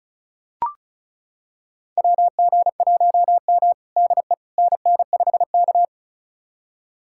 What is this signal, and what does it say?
Morse code, heard as a keyed tone, about 36 wpm, 690 Hz. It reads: WG1M DE NN5K
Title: USA Callsign DE Callsign - 4 Characters - 35wpm